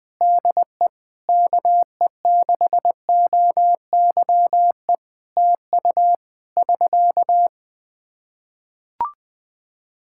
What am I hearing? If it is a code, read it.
Morse code, heard as a keyed tone, about 20 wpm, 700 Hz. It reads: DE KE6OYE TU <SK>
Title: Practice QSO - Exchange 4 - Ending - 20wpm